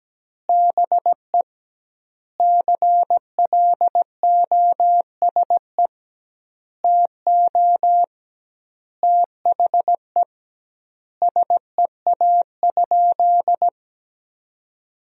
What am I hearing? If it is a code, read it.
Morse code, heard as a keyed tone, about 17 wpm, 700 Hz. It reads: BE CLOSE TO THE SEA?
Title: 2x Word-Spacing - Sentences from Top 300 Words - 17wpm